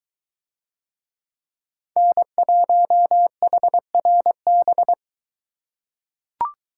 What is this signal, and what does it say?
Morse code, heard as a keyed tone, about 22 wpm, 700 Hz. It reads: N1HRB